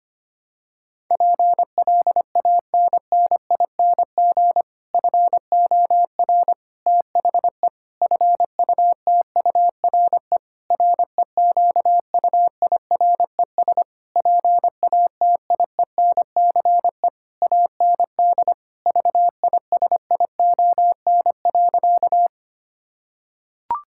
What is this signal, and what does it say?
Morse code, heard as a keyed tone, about 25 wpm, 700 Hz. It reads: PLANNING FOR THE FUTURE REQUIRES PATIENCE AND VISION.
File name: practice / 25wpm / Difficult Sentences with Hints - 25wpm